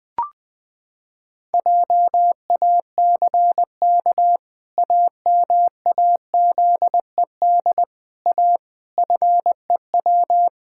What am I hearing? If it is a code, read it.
Morse code, heard as a keyed tone, about 20 wpm, 700 Hz. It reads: JACK AMAZED A FEW
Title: Pangram Challenge v2 - 20wpm